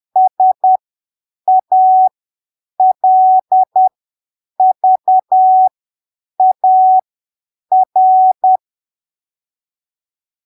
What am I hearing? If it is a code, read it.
Morse code, heard as a keyed tone, about 10 wpm, spaced as for 7 wpm, 750 Hz. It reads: SALVAR
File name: arquivos / texto / texto 03 10ppm